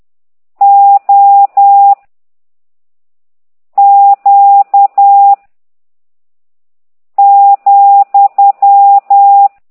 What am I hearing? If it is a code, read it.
Morse code, heard as a keyed tone, about 10 wpm, 800 Hz. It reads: O Q ,
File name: ct2idw / exerc15